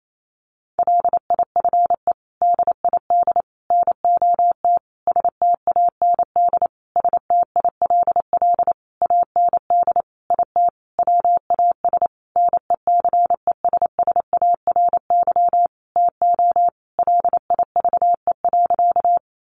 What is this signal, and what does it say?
Morse code, heard as a keyed tone, about 28 wpm, 700 Hz. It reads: LIFE DID NOT STAND STILL AND IT WAS NECESSARY TO LIVE.
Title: Sentences from Top 800 Words 28wpm